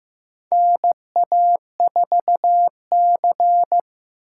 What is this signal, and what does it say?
Morse code, heard as a keyed tone, about 15 wpm, 700 Hz. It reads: NA4C
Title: K1USN SST QSO Practice - 15wpm